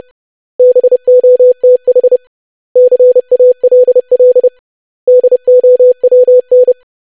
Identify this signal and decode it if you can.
Morse code, heard as a keyed tone, about 30 wpm, 500 Hz. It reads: BOTH CALL DOWN
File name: lesson / web / basic500-25wpm